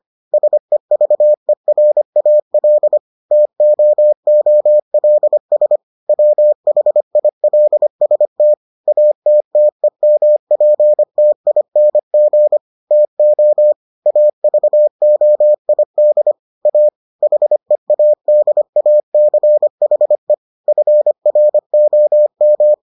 Morse code, heard as a keyed tone, about 25 wpm, 600 Hz. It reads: SEVERAL TOOLS WHILST ATTEMPTING TO AVOID A HEADACHE FROM